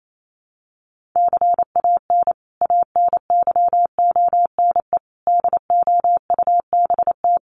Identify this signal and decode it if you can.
Morse code, heard as a keyed tone, about 28 wpm, 700 Hz. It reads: CAN ANYONE DOUBT
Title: Sentences from Top 1500 Words 28wpm